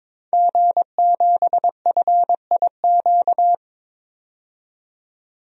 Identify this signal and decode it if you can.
Morse code, heard as a keyed tone, about 22 wpm, 700 Hz. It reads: G7FIQ
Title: DXCC-100 Call Signs - 5 Characters - 22wpm